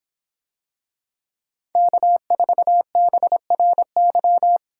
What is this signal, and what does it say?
Morse code, heard as a keyed tone, about 26 wpm, 700 Hz. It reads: K4BRY